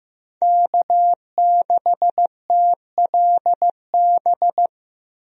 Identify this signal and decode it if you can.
Morse code, heard as a keyed tone, about 15 wpm, 700 Hz. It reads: K6TLB
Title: USA Call Signs - 5 Characters - 15wpm